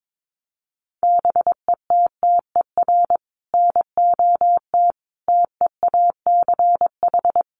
Morse code, heard as a keyed tone, about 22 wpm, 700 Hz. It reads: BETTER NOT TEACH
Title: Sentences from Top 600 Words 22wpm